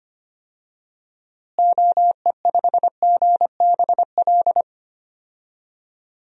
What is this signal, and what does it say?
Morse code, heard as a keyed tone, about 25 wpm, 700 Hz. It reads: OE5GBL
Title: DXCC-100 Call Signs - 6 Characters - Repeated 2x - 25wpm